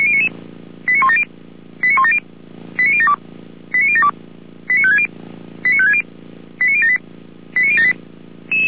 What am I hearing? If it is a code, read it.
Five-tone selective call (ZVEI), heard as quick runs of five tones, tones of about 70 ms. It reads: 89170 89170 89072 89072 89560 89560 89078 89078
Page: SDR/ZVEI